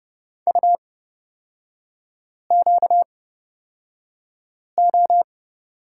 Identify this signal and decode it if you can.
Morse code, heard as a keyed tone, about 30 wpm, spaced as for 5 wpm, 700 Hz. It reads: UQO
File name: Lesson 38e - Letter q - Three Characters 30wpm